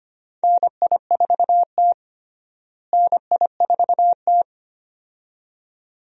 Morse code, heard as a keyed tone, about 25 wpm, 700 Hz. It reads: NI4T NI4T
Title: USA Call Signs - 4 Characters - Repeated 2x - 25wpm